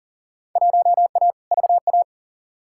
Morse code, heard as a keyed tone, about 40 wpm, 700 Hz. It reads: 1A VA
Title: Field Day Exchanges - 40wpm